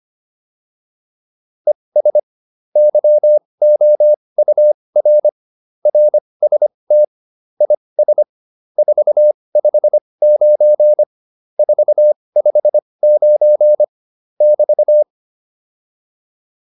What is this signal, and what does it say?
Morse code, heard as a keyed tone, about 25 wpm, spaced as for 20 wpm, 600 Hz. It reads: ES YOUR RST IS 459 459 <BT>